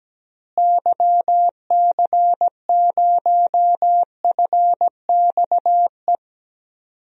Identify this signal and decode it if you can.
Morse code, heard as a keyed tone, about 17 wpm, 700 Hz. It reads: YC0FXE
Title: DXCC-100 Call Signs - 6 Characters - Repeated 2x - 17wpm